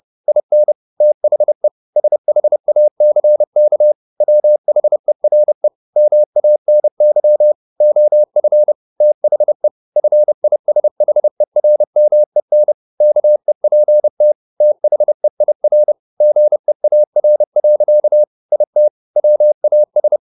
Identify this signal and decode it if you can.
Morse code, heard as a keyed tone, about 30 wpm, 600 Hz. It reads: IN THE SHACK WHERE MANY OF THE FISHERMEN KEPT THEIR GEAR. IT WAS
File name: the old man and the sea / the old man and the sea 1 30wpm